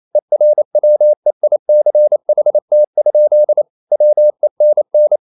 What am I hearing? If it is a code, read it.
Morse code, heard as a keyed tone, about 28 wpm, 600 Hz. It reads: ERWEICHT? WENN
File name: marienkind9 28wpm